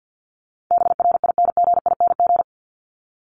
Morse code, heard as a keyed tone, about 48 wpm, 700 Hz. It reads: BRINGING
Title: ING Suffix - Top 100 Words with Base Words 50wpm